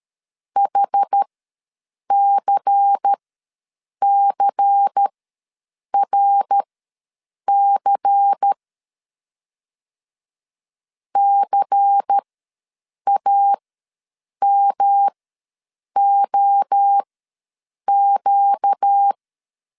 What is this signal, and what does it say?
Morse code, heard as a keyed tone, about 13 wpm, spaced as for 7 wpm, 800 Hz. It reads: HCCRC CAMOQ
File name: LEZIONI / 5 LBVC / 5 LBVC ws 12wpm 3159